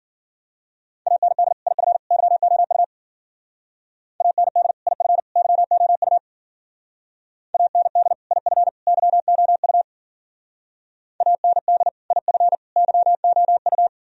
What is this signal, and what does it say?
Morse code, heard as a keyed tone, about 51 wpm, 700 Hz. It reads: AND IF YOU AND IF YOU AND IF YOU AND IF YOU